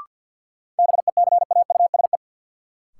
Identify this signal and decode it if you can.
Morse code, heard as a keyed tone, about 47 wpm, 700 Hz. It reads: BECAUSE
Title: Sets of 2 Words - New Word Drill - 50wpm